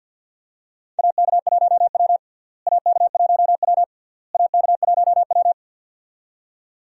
Morse code, heard as a keyed tone, about 46 wpm, 700 Hz. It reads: AK1W AK1W AK1W
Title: USA Call Signs - 4 Characters - Repeated 3x - 50wpm